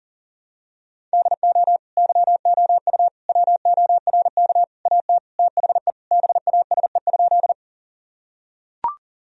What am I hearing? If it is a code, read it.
Morse code, heard as a keyed tone, about 40 wpm, 700 Hz. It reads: DO YOU WORK AT THE BASE?
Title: Sentences from Top 1000 Words - Review Part 1 - 40wpm